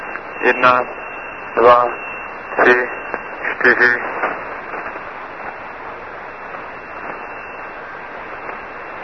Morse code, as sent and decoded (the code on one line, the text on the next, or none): none